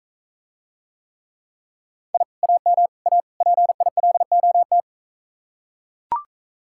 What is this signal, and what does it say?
Morse code, heard as a keyed tone, about 42 wpm, 700 Hz. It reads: I AM A PILOT